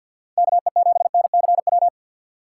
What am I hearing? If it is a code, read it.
Morse code, heard as a keyed tone, about 53 wpm, 700 Hz. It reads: KE7NCW
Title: USA Call Signs - 6 Characters - 8x Character Spacing - 50wpm